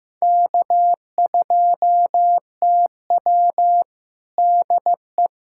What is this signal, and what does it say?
Morse code, heard as a keyed tone, about 15 wpm, 700 Hz. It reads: K2TW DE
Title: USA Callsign DE Callsign - 4 Characters - 15wpm